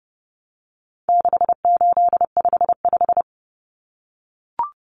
Morse code, heard as a keyed tone, about 30 wpm, 700 Hz. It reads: 6855
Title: Common Cut Numbers - 4-Digit 30wpm